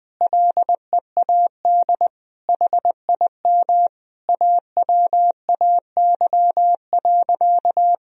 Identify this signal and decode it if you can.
Morse code, heard as a keyed tone, about 20 wpm, 700 Hz. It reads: LEAD HIM AWAY.
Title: Sentences from Top 1000 Words - Review Part 2 - 20wpm